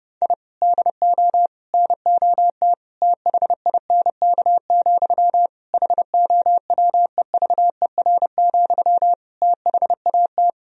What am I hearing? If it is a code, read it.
Morse code, heard as a keyed tone, about 29 wpm, 700 Hz. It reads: I DO NOT THINK, HOWEVER, THAT
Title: Sentences from Top 1000 Words - Review Part 3 - 30wpm